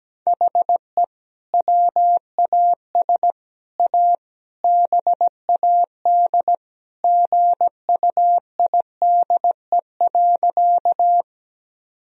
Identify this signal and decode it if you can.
Morse code, heard as a keyed tone, about 17 wpm, 700 Hz. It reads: HE WAS A BAD GUIDE.
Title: Sentences from Top 1000 Words - Review Part 3 - 17wpm